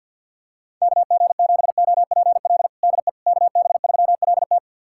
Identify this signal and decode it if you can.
Morse code, heard as a keyed tone, about 46 wpm, 700 Hz. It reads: KG7OPL DE KB3LT
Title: USA Callsign DE Callsign 50wpm